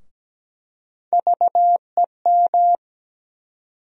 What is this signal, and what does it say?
Morse code, heard as a keyed tone, about 17 wpm, 700 Hz. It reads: VEM
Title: USA Call Sign Postfixes - 3 Letters 17wpm